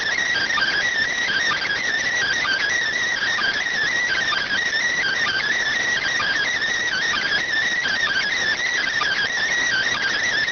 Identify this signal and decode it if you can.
Emergency Alert System (SAME) tones (header bursts), heard as harsh buzzing bursts. It is heard from 0.3 to 10.3 s.